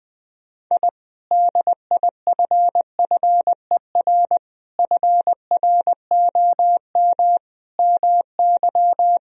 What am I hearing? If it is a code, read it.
Morse code, heard as a keyed tone, about 20 wpm, 700 Hz. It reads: I DIFFER FROM MY
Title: Sentences from Top 1000 Words - Review Part 1 - 20wpm